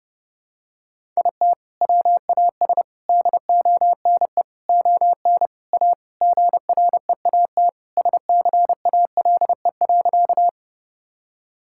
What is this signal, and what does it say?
Morse code, heard as a keyed tone, about 32 wpm, 700 Hz. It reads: IT WAS DONE ON A GREAT SCALE.
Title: Sentences from Top 1000 Words - Review Part 5 - 30wpm